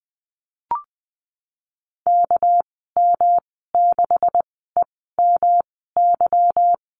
Morse code, heard as a keyed tone, about 20 wpm, spaced as for 14 wpm, 700 Hz. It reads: KM6EMY
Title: 245 Lesson 39g - Letter z - Call Signs 20wpm